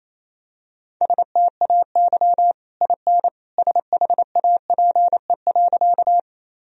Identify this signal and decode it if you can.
Morse code, heard as a keyed tone, about 28 wpm, 700 Hz. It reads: STAY IN SHAPE.
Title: Sentences from Top 1000 Words - Review Part 3 - 28wpm